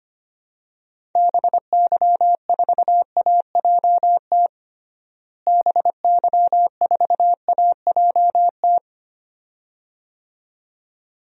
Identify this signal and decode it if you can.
Morse code, heard as a keyed tone, about 25 wpm, 700 Hz. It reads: BY4AJT BY4AJT
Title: DXCC-100 Call Signs - 6 Characters - Repeated 2x - 25wpm